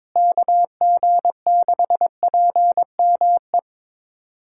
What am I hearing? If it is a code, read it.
Morse code, heard as a keyed tone, about 22 wpm, 700 Hz. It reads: KG6PME
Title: USA Callsign DE Callsign 22wpm